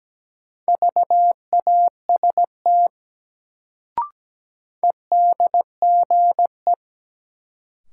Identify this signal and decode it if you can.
Morse code, heard as a keyed tone, about 17 wpm, 700 Hz. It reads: VAST EDGE
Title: Sentences from Top 1300 Words - New Word Drill - 17wpm